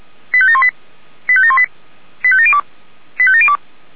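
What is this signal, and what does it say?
Five-tone selective call (ZVEI), heard as quick runs of five tones, tones of about 70 ms. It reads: 85718 85718 85702 85702
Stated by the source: ZVEI five-tone selective call received by SDR.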